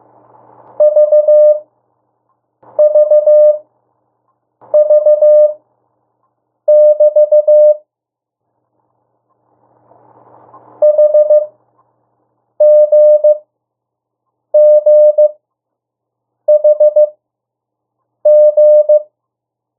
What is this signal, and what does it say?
Morse code, heard as a keyed tone, about 15 wpm, spaced as for 6 wpm, 600 Hz. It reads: VVV<BT> HGGHG
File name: morse mp3 / 02 urok 1